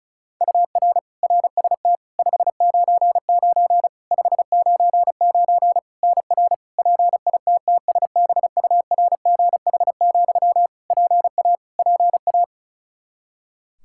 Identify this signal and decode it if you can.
Morse code, heard as a keyed tone, about 36 wpm, 700 Hz. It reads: UR RST 599 599 NR PITTSBURGH, PA PA
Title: Practice QSO - Exchange 1 - Beginning - 35wpm